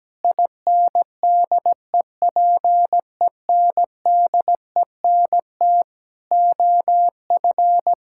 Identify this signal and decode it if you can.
Morse code, heard as a keyed tone, about 17 wpm, 700 Hz. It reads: INDEPENDENT OF
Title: Sentences from Top 1000 Words - Review Part 6 - 17wpm